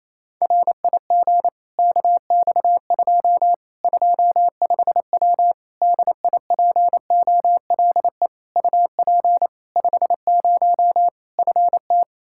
Morse code, heard as a keyed tone, about 28 wpm, 700 Hz. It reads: RIG KX2 25W DIPOLE UP 50 FT
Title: Practice QSO - Exchange 2 - Info - 28wpm